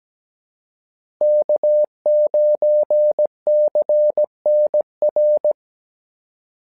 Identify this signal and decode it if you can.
Morse code, heard as a keyed tone, about 17 wpm, 600 Hz. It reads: K9CNR
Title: SOTA QSO Practice 17wpm